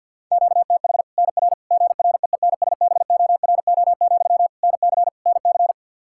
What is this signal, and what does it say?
Morse code, heard as a keyed tone, about 47 wpm, 700 Hz. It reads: QTH NR GREENSBORO, NC NC
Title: Practice QSO - Exchange 1 - Beginning - 50wpm